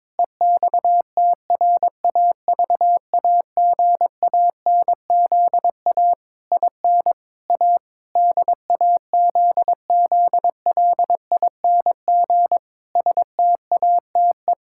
Morse code, heard as a keyed tone, about 21 wpm, 700 Hz. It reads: EXTRAVAGANZA IN A DAZZLING STATE
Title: Pangram Challenge v2 - 22wpm